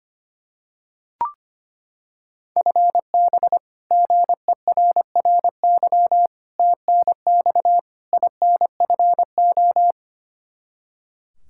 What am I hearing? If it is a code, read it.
Morse code, heard as a keyed tone, about 25 wpm, 700 Hz. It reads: FB GERRY TNX INFO
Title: Practice QSO - Exchange 3 - More Info - 25wpm